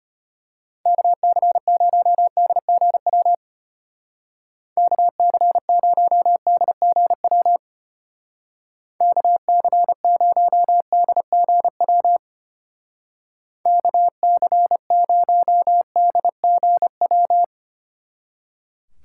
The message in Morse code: KC0DGW KC0DGW KC0DGW KC0DGW

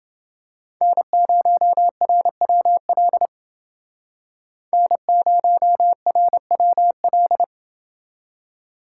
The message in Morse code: N0RWL N0RWL

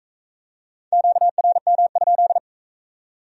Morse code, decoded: QRM?

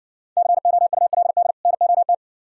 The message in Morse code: COULD NOT

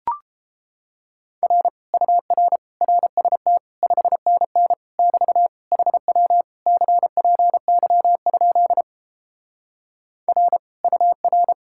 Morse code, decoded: R UR RST 5NN <BT> HW CPY? R UR

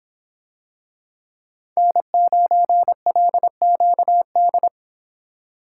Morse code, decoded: N9LQD

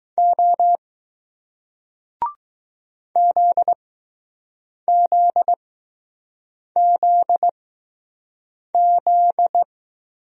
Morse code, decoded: O ZZZZ

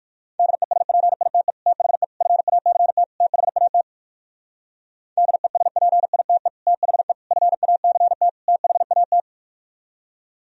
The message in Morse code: DESPITE THE FACT THAT DESPITE THE FACT THAT

